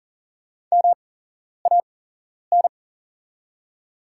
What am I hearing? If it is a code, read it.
Morse code, heard as a keyed tone, about 40 wpm, spaced as for 11 wpm, 700 Hz. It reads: MAN
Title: 8x Character Spacing - Top 300 Words - 40wpm